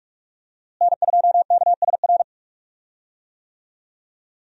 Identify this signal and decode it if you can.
Morse code, heard as a keyed tone, about 47 wpm, 700 Hz. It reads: N2KSR
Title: USA Call Signs - 5 Characters - Repeated 3x - 45wpm